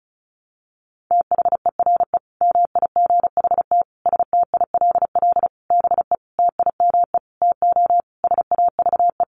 Morse code, heard as a keyed tone, about 34 wpm, 700 Hz. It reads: THERE MIGHT STILL BE TIME TO SAVE
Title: Sentences from Top 1100 Words 35wpm